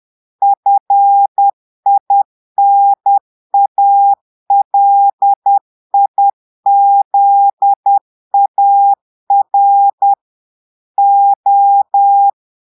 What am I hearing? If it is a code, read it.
Morse code, heard as a keyed tone, about 10 wpm, 800 Hz. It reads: FINALIZAR O
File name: portugais texte en clair 2 vitesse 10 mots minutes